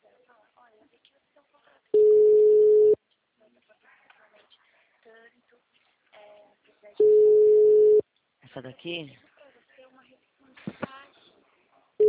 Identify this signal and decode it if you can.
Ringback tone (Europe, 425 Hz), with speech.
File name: out- -2007- - - .24000